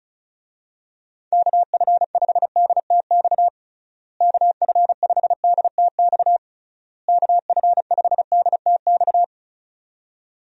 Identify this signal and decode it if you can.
Morse code, heard as a keyed tone, about 34 wpm, 700 Hz. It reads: KF5DTX KF5DTX KF5DTX
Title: USA Call Signs - 6 Characters - Repeated 3x - 35wpm